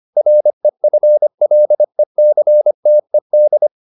Morse code, heard as a keyed tone, about 25 wpm, 600 Hz. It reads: REFLECTED